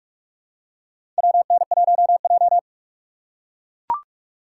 Morse code, heard as a keyed tone, about 42 wpm, 700 Hz. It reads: WN1J
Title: USA Call Signs - 3x Character Spacing - 45wpm